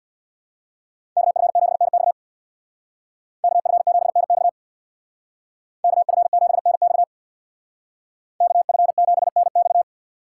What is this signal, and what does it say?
Morse code, heard as a keyed tone, about 61 wpm, 700 Hz. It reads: KF7NX KF7NX KF7NX KF7NX
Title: Speed Racing - USA Call Signs 50wpm